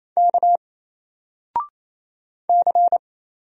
Morse code, heard as a keyed tone, about 27 wpm, 700 Hz. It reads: K C
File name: Single-Letter-Number-28wpm